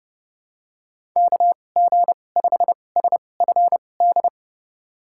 Morse code, heard as a keyed tone, about 30 wpm, spaced as for 22 wpm, 700 Hz. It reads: KG5SFD
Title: USA Call Signs - 6 Characters - 2x Character Spacing - 30wpm